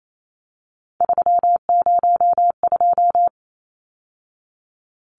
Morse code, heard as a keyed tone, about 28 wpm, 700 Hz. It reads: 302
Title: Numbers - 3-Digit 28wpm